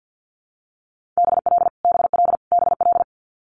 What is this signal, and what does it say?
Morse code, heard as a keyed tone, about 50 wpm, 700 Hz. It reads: BL BL BL
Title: Multi-letter Phonemes Repeated 3x 50wpm